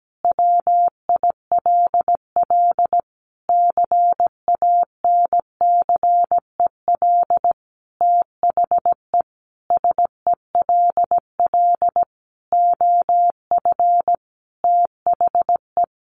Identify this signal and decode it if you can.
Morse code, heard as a keyed tone, about 17 wpm, 700 Hz. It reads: WILL CANCEL THE SELL OF THE
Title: Sentences from Top 1200 Words 17wpm